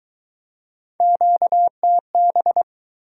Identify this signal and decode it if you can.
Morse code, heard as a keyed tone, about 23 wpm, 700 Hz. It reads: QTB